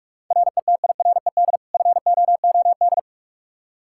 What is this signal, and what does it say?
Morse code, heard as a keyed tone, about 45 wpm, 700 Hz. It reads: RETIRED FOOD
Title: Practice QSO - Exchange 3 - More Info - 45wpm